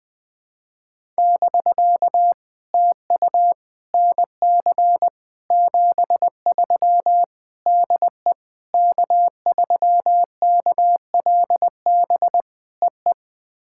Morse code, heard as a keyed tone, about 20 wpm, 700 Hz. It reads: <BK> TU NC 73 DE K3KLB EE
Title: POTA QSO Practice - 20wpm